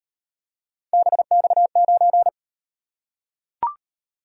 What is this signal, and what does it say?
Morse code, heard as a keyed tone, about 37 wpm, 700 Hz. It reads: BX9